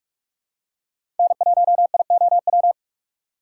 Morse code, heard as a keyed tone, about 46 wpm, 700 Hz. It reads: N1IOW
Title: USA Call Signs - 5 Characters - 3x Character Spacing - 45wpm